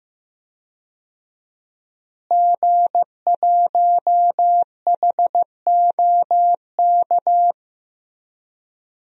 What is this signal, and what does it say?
Morse code, heard as a keyed tone, about 15 wpm, 700 Hz. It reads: G1HOK
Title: DXCC-100 Call Signs - 5 Characters - Repeated 3x - 15wpm